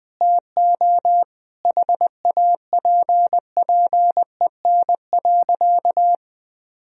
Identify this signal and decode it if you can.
Morse code, heard as a keyed tone, about 20 wpm, 700 Hz. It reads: TO HAPPEN.
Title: Sentences from Top 1000 Words - Review Part 4 - 20wpm